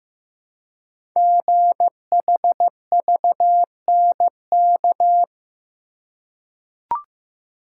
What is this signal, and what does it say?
Morse code, heard as a keyed tone, about 15 wpm, 700 Hz. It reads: GHVNK
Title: Code Groups - 5 Letters - 15wpm